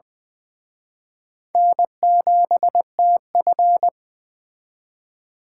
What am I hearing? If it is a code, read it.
Morse code, heard as a keyed tone, about 24 wpm, 700 Hz. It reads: N7TF
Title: Speed Racing - USA 1x2 Call Signs 20wpm